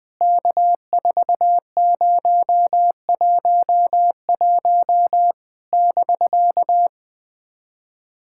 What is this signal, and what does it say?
Morse code, heard as a keyed tone, about 19 wpm, 700 Hz. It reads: K4011 <BK>
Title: POTA QSO Practice - 20wpm